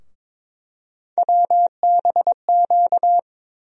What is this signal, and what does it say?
Morse code, heard as a keyed tone, about 22 wpm, 700 Hz. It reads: WBQ
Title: USA Call Sign Postfixes - 3 Letters 22wpm